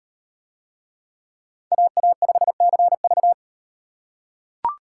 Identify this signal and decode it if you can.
Morse code, heard as a keyed tone, about 38 wpm, 700 Hz. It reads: AA5CV